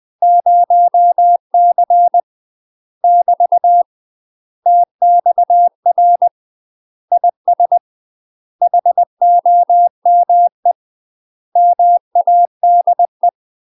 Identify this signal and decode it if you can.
Morse code, heard as a keyed tone, about 20 wpm, 700 Hz. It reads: 0C <BT> TXR IS HOME MADE
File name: UK MASTER QSOs TRAINING / 20WPM / UK MASTER QSOs 0044